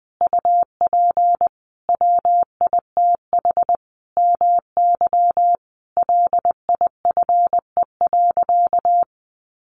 Morse code, heard as a keyed tone, about 20 wpm, 700 Hz. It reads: UP WITH MY LIFE.
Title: Sentences from Top 1500 Words 20wpm